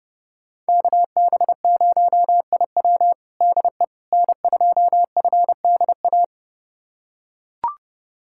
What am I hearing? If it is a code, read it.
Morse code, heard as a keyed tone, about 30 wpm, 700 Hz. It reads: KB0IW DE N2FDA
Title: USA Callsign DE Callsign 30wpm